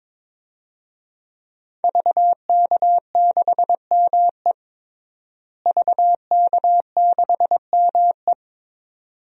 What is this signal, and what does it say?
Morse code, heard as a keyed tone, about 21 wpm, 700 Hz. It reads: VK6ME VK6ME
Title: DXCC-100 Call Signs - 5 Characters - Repeated 2x - 22wpm